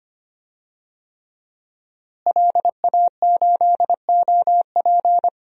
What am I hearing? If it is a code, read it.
Morse code, heard as a keyed tone, about 25 wpm, 700 Hz. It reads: LA8OP
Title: DXCC-100 Call Signs - 5 Characters - Repeated 3x - 25wpm